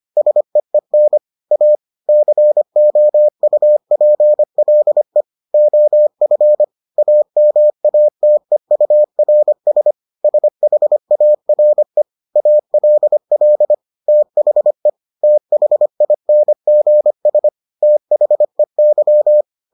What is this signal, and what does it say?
Morse code, heard as a keyed tone, about 25 wpm, 600 Hz. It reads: SEEN A COUPLE OF AMATEURS SHARE ALL THE THINGS THEY